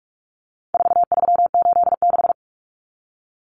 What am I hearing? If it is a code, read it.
Morse code, heard as a keyed tone, about 44 wpm, 700 Hz. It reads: 4386
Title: Common Cut Numbers - 4-Digit 45wpm